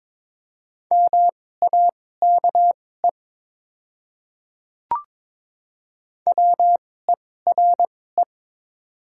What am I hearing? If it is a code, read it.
Morse code, heard as a keyed tone, about 22 wpm, spaced as for 16 wpm, 700 Hz. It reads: MAKE WERE
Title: 2x Character Spacing - Top 100 Words - 22wpm